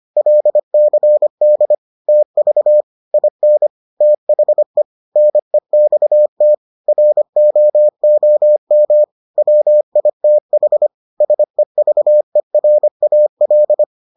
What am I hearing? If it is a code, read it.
Morse code, heard as a keyed tone, about 25 wpm, 600 Hz. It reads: LCD TV IN THE NEXT ROOM WITH SEVERAL